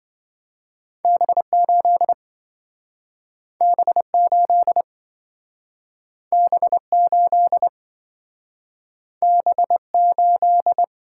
Morse code, heard as a keyed tone, about 26 wpm, 700 Hz. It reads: B8 B8 B8 B8